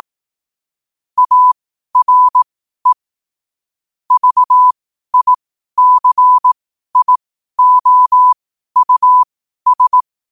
Morse code, heard as a keyed tone, about 18 wpm, spaced as for 12 wpm, 1000 Hz. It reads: ARE VICIOUS
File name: qotdfiles / 12wpm